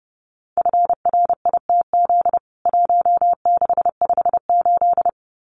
Morse code, heard as a keyed tone, about 30 wpm, 700 Hz. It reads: FRITZ 1658
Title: CWT Exchange 30wpm